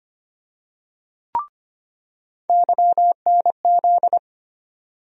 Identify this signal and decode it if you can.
Morse code, heard as a keyed tone, about 25 wpm, 700 Hz. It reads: YNZ
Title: USA Call Sign Postfixes - 3 Letters 25wpm